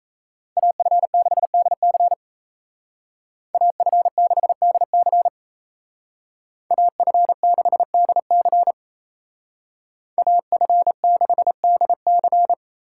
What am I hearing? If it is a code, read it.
Morse code, heard as a keyed tone, about 36 wpm, 700 Hz. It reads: AF6DC AF6DC AF6DC AF6DC